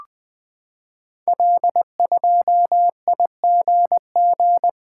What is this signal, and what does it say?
Morse code, heard as a keyed tone, about 20 wpm, 700 Hz. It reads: L2IGG
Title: Code Groups - 5 Letter:Numbers - 20wpm